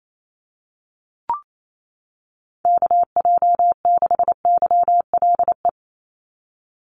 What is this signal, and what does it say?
Morse code, heard as a keyed tone, about 28 wpm, 700 Hz. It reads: KJ6YLE
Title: USA Call Signs 28wpm